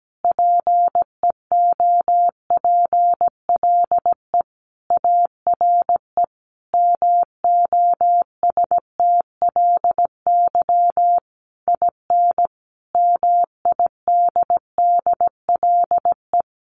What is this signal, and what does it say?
Morse code, heard as a keyed tone, about 17 wpm, 700 Hz. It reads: PEOPLE ARE MOSTLY IN MIDDLE
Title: Sentences from Top 1300 Words 17wpm